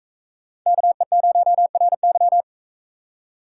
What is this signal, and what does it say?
Morse code, heard as a keyed tone, about 44 wpm, 700 Hz. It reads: KE0RY